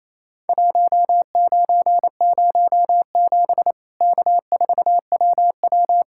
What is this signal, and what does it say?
Morse code, heard as a keyed tone, about 28 wpm, 700 Hz. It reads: 1907 K4WW
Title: Worked All Europe DX Contest - QTC Exchange - 28wpm